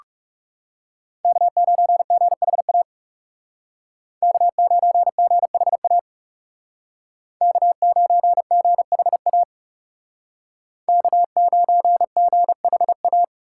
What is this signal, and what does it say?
Morse code, heard as a keyed tone, about 38 wpm, 700 Hz. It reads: K9GHA K9GHA K9GHA K9GHA